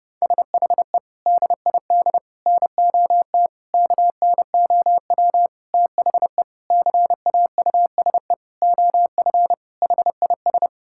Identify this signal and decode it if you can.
Morse code, heard as a keyed tone, about 30 wpm, 700 Hz. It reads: SHE DID NOT KNOW THE CAUSE OF HIS
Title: Sentences from Top 1600 Words 30wpm